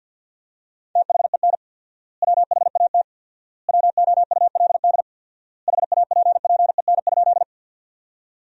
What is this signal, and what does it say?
Morse code, heard as a keyed tone, about 51 wpm, 700 Hz. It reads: THEN WHAT WOULD HAPPEN?
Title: 4x Word-Spacing - Sentences from Top 400 Words - 50wpm